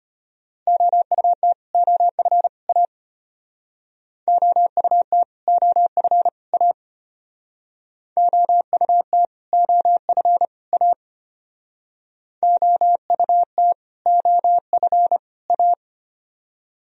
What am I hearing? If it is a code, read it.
Morse code, heard as a keyed tone, about 32 wpm, 700 Hz. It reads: OUT OF A OUT OF A OUT OF A OUT OF A